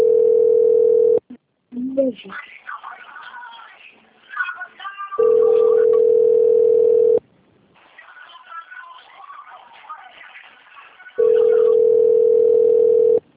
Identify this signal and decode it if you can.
Ringback tone (North America), 440 + 480 Hz, with speech.
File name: out- - - - - .4994